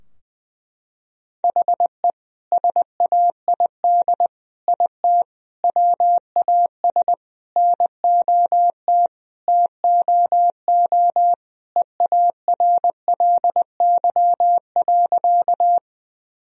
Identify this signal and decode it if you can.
Morse code, heard as a keyed tone, about 20 wpm, 700 Hz. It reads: HE SAID IT WAS NOT TOO EARLY.
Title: Sentences from Top 1000 Words - Review Part 2 - 20wpm